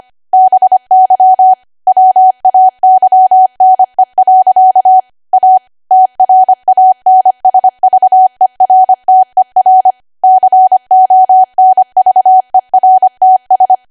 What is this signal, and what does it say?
Morse code, heard as a keyed tone, about 25 wpm, 750 Hz. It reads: BY WAYNE. A TRANSVERTER CONVERTS
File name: Archive / 25 WPM / 25WPM